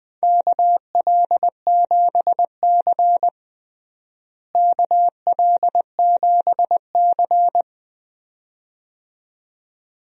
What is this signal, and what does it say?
Morse code, heard as a keyed tone, about 20 wpm, 700 Hz. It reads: KL7C KL7C
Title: DXCC-100 Call Signs - 4 Characters - Repeated 2x - 20wpm